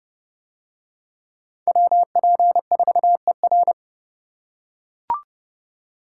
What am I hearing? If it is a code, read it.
Morse code, heard as a keyed tone, about 29 wpm, 700 Hz. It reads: WP4ER